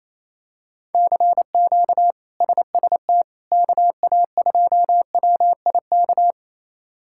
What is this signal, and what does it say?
Morse code, heard as a keyed tone, about 28 wpm, 700 Hz. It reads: CQ SST KA2WIK
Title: K1USN SST QSO Practice - 28wpm